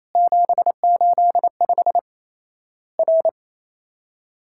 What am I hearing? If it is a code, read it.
Morse code, heard as a keyed tone, about 28 wpm, 700 Hz. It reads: 785 R
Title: Worked All Europe DX Contest - QTC Exchange - 28wpm